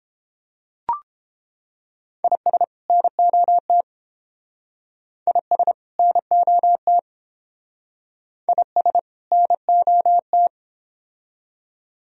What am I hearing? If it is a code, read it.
Morse code, heard as a keyed tone, about 30 wpm, 700 Hz. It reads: IS NOT IS NOT IS NOT